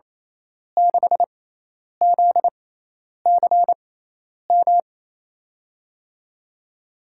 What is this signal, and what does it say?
Morse code, heard as a keyed tone, about 28 wpm, spaced as for 10 wpm, 700 Hz. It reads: 6ZCM
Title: USA Call Signs - 6x Character Spacing - 28wpm